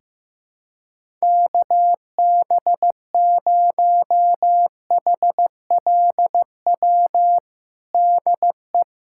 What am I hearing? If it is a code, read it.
Morse code, heard as a keyed tone, about 15 wpm, 700 Hz. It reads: KB0HLW DE
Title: USA Callsign DE Callsign - 6 Characters - 15wpm